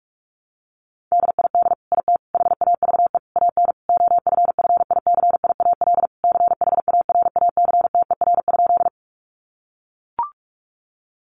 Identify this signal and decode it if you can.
Morse code, heard as a keyed tone, about 44 wpm, 700 Hz. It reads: DID IT HAVE AN OFFICIAL CHARACTER?
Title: Sentences from Top 1200 Words 45wpm